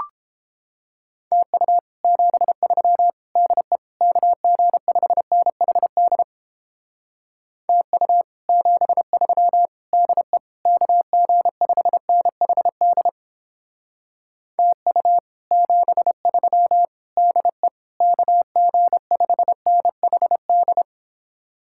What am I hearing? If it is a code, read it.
Morse code, heard as a keyed tone, about 29 wpm, 700 Hz. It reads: TU 73 DE KG5NHD TU 73 DE KG5NHD TU 73 DE KG5NHD